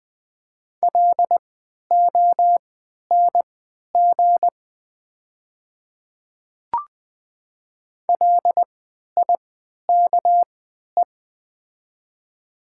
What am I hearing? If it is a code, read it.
Morse code, heard as a keyed tone, about 20 wpm, spaced as for 11 wpm, 700 Hz. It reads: LONG LIKE
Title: 3x Character Spacing - Top 100 Words - 20wpm